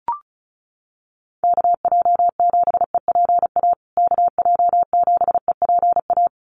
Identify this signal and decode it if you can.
Morse code, heard as a keyed tone, about 34 wpm, 700 Hz. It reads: KJ7EPA KJ7EPA
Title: USA Call Signs Repeated 2x 35wpm